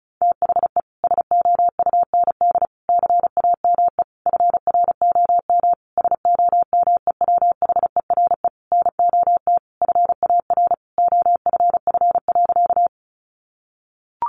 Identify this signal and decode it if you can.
Morse code, heard as a keyed tone, about 34 wpm, 700 Hz. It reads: THE SOUND CAME FROM SOMEWHERE NOT FAR OFF.
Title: Sentences from Top 1400 Words 35wpm